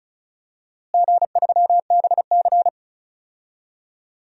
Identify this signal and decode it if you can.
Morse code, heard as a keyed tone, about 34 wpm, 700 Hz. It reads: G3BC
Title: DXCC-100 Call Signs - 4 Characters - Repeated 2x - 35wpm